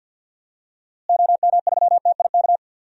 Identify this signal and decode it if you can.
Morse code, heard as a keyed tone, about 47 wpm, 700 Hz. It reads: OM3TIK
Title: DXCC-100 Call Signs - 6 Characters - Repeated 3x - 50wpm